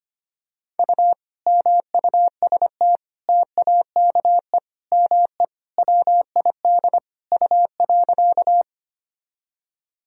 Morse code, heard as a keyed tone, about 25 wpm, 700 Hz. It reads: U MUST TAKE ME WID U.
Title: Sentences from Top 200 Words and Abbreviations 25wpm